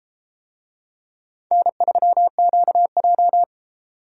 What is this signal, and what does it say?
Morse code, heard as a keyed tone, about 32 wpm, 700 Hz. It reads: N3QJ